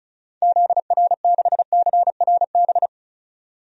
Morse code, heard as a keyed tone, about 36 wpm, 700 Hz. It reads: ZR6CRB
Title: DXCC-100 Call Signs - 6 Characters - 35wpm